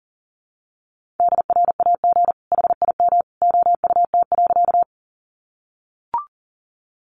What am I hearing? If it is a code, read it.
Morse code, heard as a keyed tone, about 40 wpm, 700 Hz. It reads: DRAG HIM OUT.
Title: Sentences from Top 1500 Words 40wpm